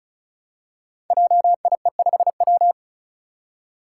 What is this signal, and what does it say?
Morse code, heard as a keyed tone, about 36 wpm, 700 Hz. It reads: JIE5W
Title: Code Groups - 5 Letter:Numbers - 35wpm